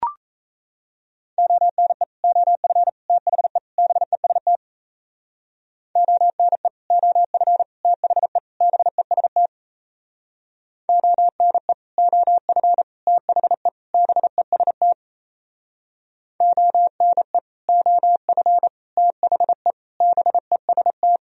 Morse code, heard as a keyed tone, about 34 wpm, 700 Hz. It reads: ONE OF THE BEST ONE OF THE BEST ONE OF THE BEST ONE OF THE BEST
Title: Speed-Racing - Sets of 4 Words 28wpm